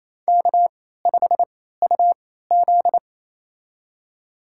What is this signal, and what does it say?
Morse code, heard as a keyed tone, about 28 wpm, spaced as for 16 wpm, 700 Hz. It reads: K5UZ
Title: USA Call Signs - 4 Characters - 3x Character Spacing - 28wpm